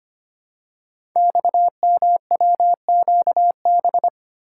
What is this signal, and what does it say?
Morse code, heard as a keyed tone, about 25 wpm, 700 Hz. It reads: XMWQB
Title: Code Groups - 5 Letter:Numbers - 25wpm